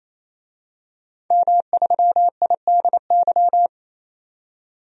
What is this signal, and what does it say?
Morse code, heard as a keyed tone, about 28 wpm, 700 Hz. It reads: M3IDY
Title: DXCC-100 Call Signs - 5 Characters - 28wpm